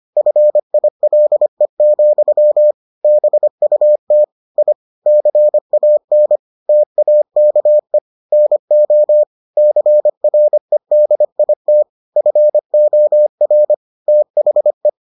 Morse code, heard as a keyed tone, about 25 wpm, 600 Hz. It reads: FILE, BUT I CAN TAKE NO CREDIT FOR THE